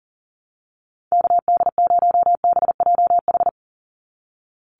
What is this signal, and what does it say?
Morse code, heard as a keyed tone, about 40 wpm, 700 Hz. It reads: KD0BJH
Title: USA Call Signs 40wpm